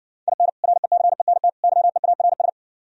Morse code, heard as a keyed tone, about 67 wpm, 700 Hz. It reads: IN RECENT YEARS